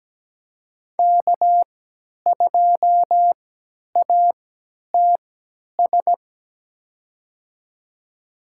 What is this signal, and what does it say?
Morse code, heard as a keyed tone, about 17 wpm, spaced as for 10 wpm, 700 Hz. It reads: K2ATS
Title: USA Call Signs - 3x Character Spacing - 17wpm